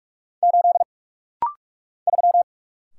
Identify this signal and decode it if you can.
Morse code, heard as a keyed tone, about 44 wpm, 700 Hz. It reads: 8 3
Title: Single Letter-Number - Rapid-Fire 45wpm